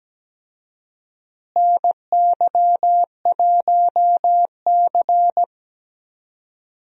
Morse code, heard as a keyed tone, about 17 wpm, 700 Hz. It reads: NY1C
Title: USA Call Signs - 4 Characters - Repeated 2x - 17wpm